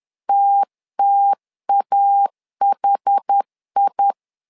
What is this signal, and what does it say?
Morse code, heard as a keyed tone, about 10 wpm, 800 Hz. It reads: TTAHI
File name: LEZIONI / 2 UHDN / 2 UH 10wpm 1714